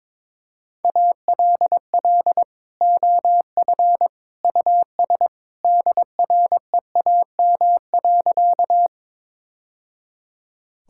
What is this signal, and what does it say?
Morse code, heard as a keyed tone, about 21 wpm, 700 Hz. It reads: ALL OF US DREAM.
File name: practice / 22wpm / Sentences with 2-Word Phrases - 22wpm